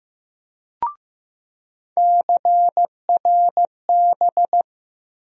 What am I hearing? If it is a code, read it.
Morse code, heard as a keyed tone, about 15 wpm, 700 Hz. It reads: CRB
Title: USA Call Sign Postfixes - 3 Letters 15wpm